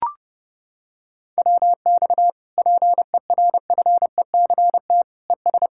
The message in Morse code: WX PERFECT ES

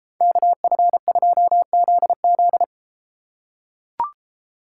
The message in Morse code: KF2ZZ

